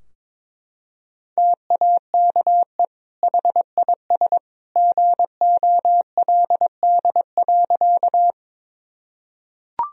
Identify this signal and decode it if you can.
Morse code, heard as a keyed tone, about 22 wpm, 700 Hz. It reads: TAKE HIS GOLD.
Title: Sentences from Top 1000 Words - Review Part 3 - 22wpm